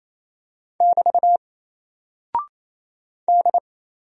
Morse code, heard as a keyed tone, about 27 wpm, 700 Hz. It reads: <BT> D
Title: Single Character (Letter:Number:Punctuation:Prosign) 28wpm